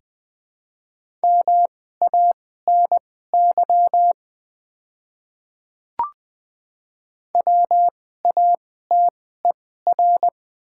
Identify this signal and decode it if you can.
Morse code, heard as a keyed tone, about 21 wpm, spaced as for 15 wpm, 700 Hz. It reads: MANY WATER
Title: 2x Character Spacing - Top 200 Words - 20wpm